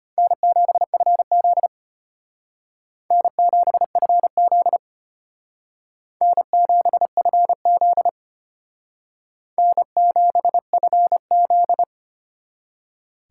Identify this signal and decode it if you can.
Morse code, heard as a keyed tone, about 32 wpm, 700 Hz. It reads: N7FZ N7FZ N7FZ N7FZ